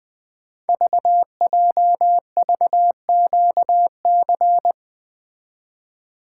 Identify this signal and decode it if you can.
Morse code, heard as a keyed tone, about 20 wpm, 700 Hz. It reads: VJVQC
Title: Code Groups - 5 Letters - 20wpm